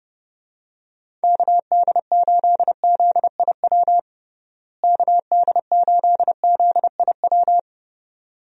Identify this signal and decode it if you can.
Morse code, heard as a keyed tone, about 29 wpm, 700 Hz. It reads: KD8ZIW KD8ZIW
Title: USA Call Signs - 6 Characters - Repeated 2x - 30wpm